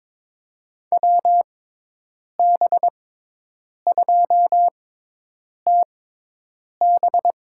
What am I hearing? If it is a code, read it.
Morse code, heard as a keyed tone, about 22 wpm, spaced as for 8 wpm, 700 Hz. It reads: WB2TB
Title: USA Call Signs - 6x Character Spacing - 22wpm